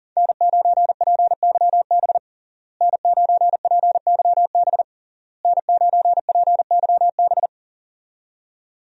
Morse code, heard as a keyed tone, about 37 wpm, 700 Hz. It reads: N9PYB N9PYB N9PYB
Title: USA Call Signs - 5 Characters - Repeated 3x - 40wpm